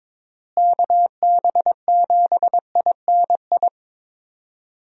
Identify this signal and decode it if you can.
Morse code, heard as a keyed tone, about 22 wpm, 700 Hz. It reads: KB7INI
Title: USA Call Signs - 6 Characters - Repeated 2x - 22wpm